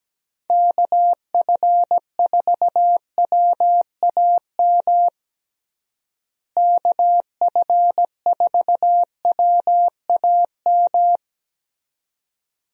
Morse code, heard as a keyed tone, about 17 wpm, 700 Hz. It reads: KF4WAM KF4WAM
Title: USA Call Signs - 6 Characters - Repeated 2x - 17wpm